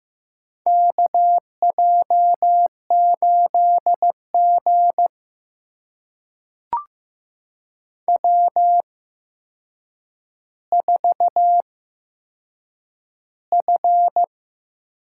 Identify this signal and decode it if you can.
Morse code, heard as a keyed tone, about 15 wpm, spaced as for 7 wpm, 700 Hz. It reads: KJ8G W4F
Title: USA Call Signs - 4 Characters - 8x Character Spacing - 15wpm